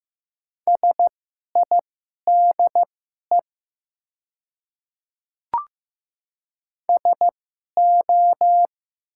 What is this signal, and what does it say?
Morse code, heard as a keyed tone, about 15 wpm, spaced as for 11 wpm, 700 Hz. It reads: SIDE SO
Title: 2x Character Spacing - Top 100 Words - 15wpm